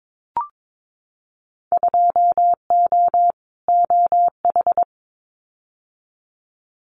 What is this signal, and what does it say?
Morse code, heard as a keyed tone, about 22 wpm, 700 Hz. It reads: 2O OH
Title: Winter Field Day Exchanges 22wpm